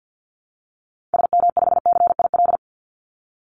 Morse code, heard as a keyed tone, about 50 wpm, 700 Hz. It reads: SM5CIL
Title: Call Signs - 50wpm